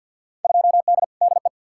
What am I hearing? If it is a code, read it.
Morse code, heard as a keyed tone, about 53 wpm, 700 Hz. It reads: 2D DE
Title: Field Day Exchanges Encore - 50wpm